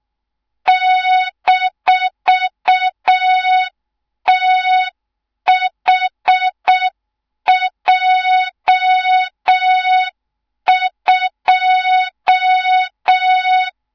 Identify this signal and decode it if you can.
Morse code, heard as a keyed tone, about 6 wpm, 3700 Hz. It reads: -THJ2
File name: Morze-HA2MN / hang / 24 GYAKORLAT